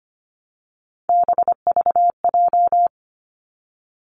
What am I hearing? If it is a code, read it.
Morse code, heard as a keyed tone, about 25 wpm, 700 Hz. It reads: BVJ
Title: Three Challenging Letter-Number Combinations 25wpm